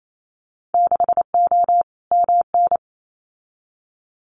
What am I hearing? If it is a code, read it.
Morse code, heard as a keyed tone, about 28 wpm, 700 Hz. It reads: 6O MN
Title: Winter Field Day Exchanges Encore 28wpm